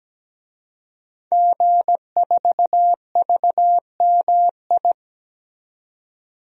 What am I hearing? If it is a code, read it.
Morse code, heard as a keyed tone, about 17 wpm, 700 Hz. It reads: G4VMI
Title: DXCC-100 Call Signs - 5 Characters - 17wpm